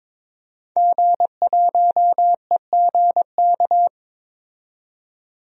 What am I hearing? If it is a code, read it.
Morse code, heard as a keyed tone, about 22 wpm, 700 Hz. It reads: G1EGK
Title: DXCC-100 Call Signs - 5 Characters - 22wpm